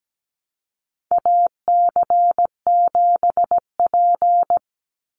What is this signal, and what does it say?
Morse code, heard as a keyed tone, about 17 wpm, 700 Hz. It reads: AC7P
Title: NA SOTA Chaser Callsigns 17wpm